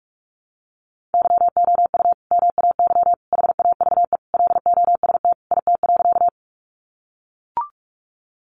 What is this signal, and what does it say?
Morse code, heard as a keyed tone, about 44 wpm, 700 Hz. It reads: YOU MAY HAVE LOST IT.
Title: Sentences from Top 1400 Words 45wpm